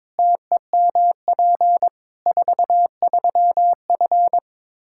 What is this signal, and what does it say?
Morse code, heard as a keyed tone, about 22 wpm, 700 Hz. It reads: TEMP 43F
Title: K1USN SST QSO Practice 22wpm